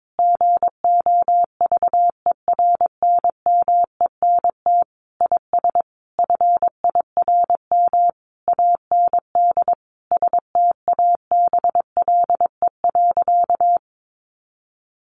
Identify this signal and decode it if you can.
Morse code, heard as a keyed tone, about 22 wpm, 700 Hz. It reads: GOVERNMENT IS FIRM AND STABLE.
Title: Sentences from Top 1500 Words 22wpm